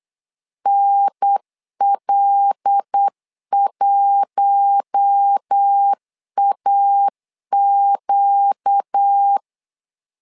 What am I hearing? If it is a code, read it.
Morse code, heard as a keyed tone, about 8 wpm, 800 Hz. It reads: NL1AQ
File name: LEZIONI / 5 LBVC / 5 LBVC 8wpm 2935